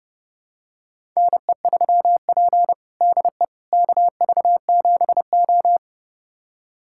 Morse code, heard as a keyed tone, about 32 wpm, 700 Hz. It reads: NE3P DE KV7O
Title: USA Callsign DE Callsign - 4 Characters - 30wpm